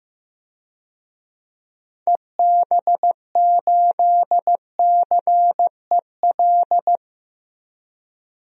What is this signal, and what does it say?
Morse code, heard as a keyed tone, about 15 wpm, 700 Hz. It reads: EB8CEL
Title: DXCC-100 Call Signs - 6 Characters - 15wpm